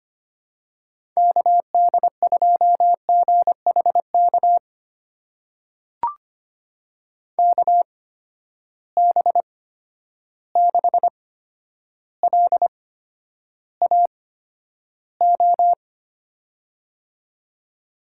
Morse code, heard as a keyed tone, about 25 wpm, spaced as for 11 wpm, 700 Hz. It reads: KD2GHK KB6LAO
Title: USA Call Signs - 6 Characters - 8x Character Spacing - 25wpm